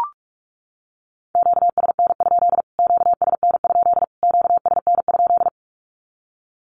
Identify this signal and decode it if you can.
Morse code, heard as a keyed tone, about 44 wpm, 700 Hz. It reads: QSN? QSN? QSN?
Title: Q-Codes Repeated 3x 45wpm